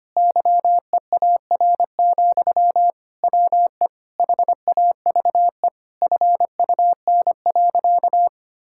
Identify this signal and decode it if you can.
Morse code, heard as a keyed tone, about 25 wpm, 700 Hz. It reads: YEAR, WE HAVE FUN.
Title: Sentences from Top 1000 Words - Review Part 5 - 25wpm